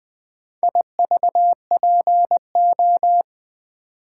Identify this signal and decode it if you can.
Morse code, heard as a keyed tone, about 20 wpm, 700 Hz. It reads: IVPO
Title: Four Letter-Number Combinations - 20wpm